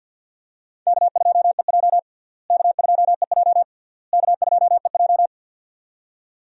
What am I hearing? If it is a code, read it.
Morse code, heard as a keyed tone, about 51 wpm, 700 Hz. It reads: K2EJ K2EJ K2EJ
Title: USA Call Signs - 4 Characters - Repeated 3x - 50wpm